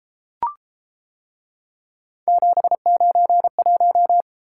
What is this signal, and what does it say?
Morse code, heard as a keyed tone, about 31 wpm, 700 Hz. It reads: 791